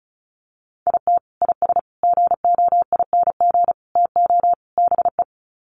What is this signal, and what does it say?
Morse code, heard as a keyed tone, about 34 wpm, 700 Hz. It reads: IT IS GOING TO BE
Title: Sets of 5 Words 35wpm